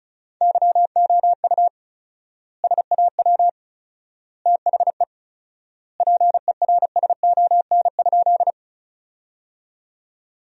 Morse code, heard as a keyed tone, about 36 wpm, 700 Hz. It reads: YOU SAW THE PERSON?
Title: 4x Word-Spacing - Sentences from Top 400 Words - 35wpm